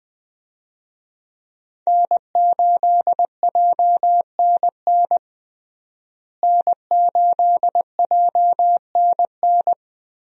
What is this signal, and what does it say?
Morse code, heard as a keyed tone, about 20 wpm, 700 Hz. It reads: N8JNN N8JNN
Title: USA Call Signs - 5 Characters - Repeated 2x - 20wpm